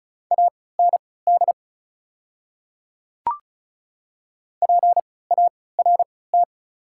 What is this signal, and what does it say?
Morse code, heard as a keyed tone, about 36 wpm, spaced as for 20 wpm, 700 Hz. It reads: AND PART
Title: 3x Character Spacing - Top 300 Words - 35wpm